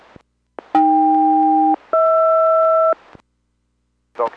Aircraft SELCAL, heard as two pairs of tones. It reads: AK-HR